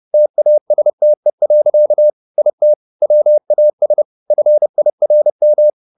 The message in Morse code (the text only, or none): TASTE. IT WAS FIRM